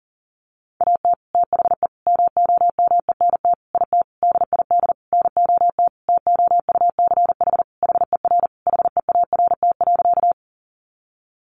Morse code, heard as a keyed tone, about 40 wpm, 700 Hz. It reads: AT THE MOMENT IT DID NOT TOUCH HER HEART.
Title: Sentences from Top 1200 Words 40wpm